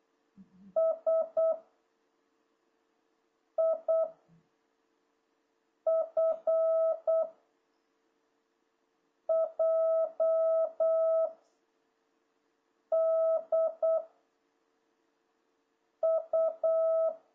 Morse code, heard as a keyed tone, about 8 wpm, 650 Hz. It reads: S I F J D U
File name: jh8cbh / 2026cw